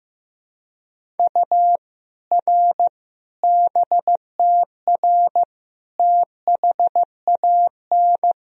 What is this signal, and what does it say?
Morse code, heard as a keyed tone, about 16 wpm, 700 Hz. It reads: U R BTR THAN
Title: Sentences from Top 400 Words and Abbreviations 15wpm